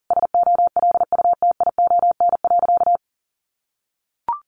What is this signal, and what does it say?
Morse code, heard as a keyed tone, about 40 wpm, 700 Hz. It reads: SOLUTION.
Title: Sentences from Top 1300 Words 40wpm